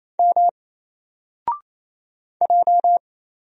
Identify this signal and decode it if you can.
Morse code, heard as a keyed tone, about 28 wpm, 700 Hz. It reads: M J
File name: Single-Letter-Number-28wpm